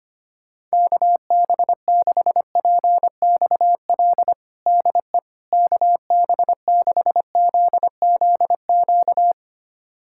Morse code, heard as a keyed tone, about 25 wpm, 700 Hz. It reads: KB6PXL DE KB6ZZQ
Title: USA Callsign DE Callsign - 6 Characters - 25wpm